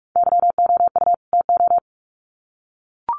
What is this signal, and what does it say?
Morse code, heard as a keyed tone, about 44 wpm, 700 Hz. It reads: YOU TO
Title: Sets of 2 Words 45wpm